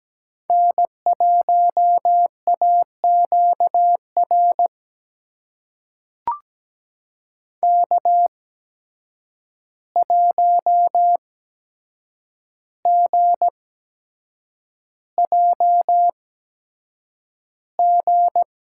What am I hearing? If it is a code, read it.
Morse code, heard as a keyed tone, about 17 wpm, spaced as for 7 wpm, 700 Hz. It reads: N1AQR K1GJG
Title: USA Call Signs - 5 Characters - 8x Character Spacing - 17wpm